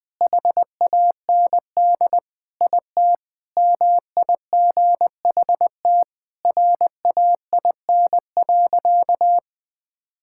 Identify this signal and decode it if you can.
Morse code, heard as a keyed tone, about 21 wpm, 700 Hz. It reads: HAND IT MIGHT RAIN.
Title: Sentences with 3-Word Phrases - 20wpm